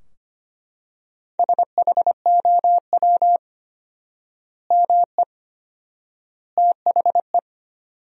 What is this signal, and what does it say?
Morse code, heard as a keyed tone, about 25 wpm, 700 Hz. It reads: SHOW ME THE
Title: 4x Word-Spacing - Sentences from Top 200 Words - 25wpm